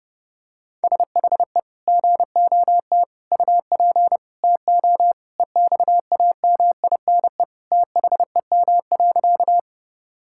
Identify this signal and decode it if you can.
Morse code, heard as a keyed tone, about 32 wpm, 700 Hz. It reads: SHE GOT UP TO EXAMINE THEM.
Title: Sentences from Top 1600 Words 30wpm